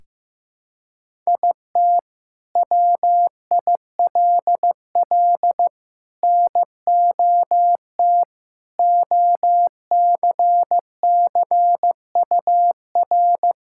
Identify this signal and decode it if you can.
Morse code, heard as a keyed tone, about 15 wpm, 700 Hz. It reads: IT WILL NOT OCCUR
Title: Sentences from Top 1000 Words - Review Part 6 - 15wpm